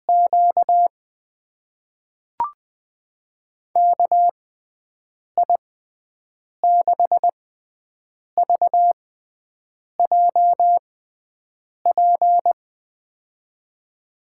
Morse code, heard as a keyed tone, about 20 wpm, spaced as for 7 wpm, 700 Hz. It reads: Q KI6VJP
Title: USA Call Signs - 6 Characters - 6x Character Spacing - 20wpm